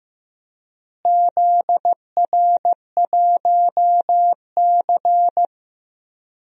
Morse code, heard as a keyed tone, about 15 wpm, 700 Hz. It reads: ZR1C
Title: DXCC-100 Call Signs - 4 Characters - Repeated 3x - 15wpm